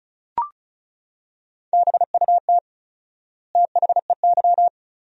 Morse code, heard as a keyed tone, about 36 wpm, 700 Hz. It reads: BUT THEY
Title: 4x Word-Spacing - Sentences from Top 500 Words - 35wpm